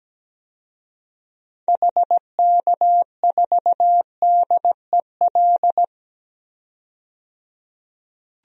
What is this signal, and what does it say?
Morse code, heard as a keyed tone, about 17 wpm, 700 Hz. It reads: HK4DEL
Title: DXCC-100 Call Signs - 6 Characters - 17wpm